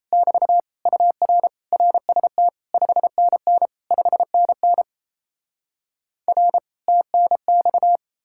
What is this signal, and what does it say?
Morse code, heard as a keyed tone, about 38 wpm, 700 Hz. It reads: <BT> UR RST 5NN 5NN R TNX